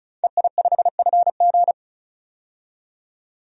Morse code, heard as a keyed tone, about 34 wpm, 700 Hz. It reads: EI5FG
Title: DXCC-100 Call Signs - 5 Characters - 35wpm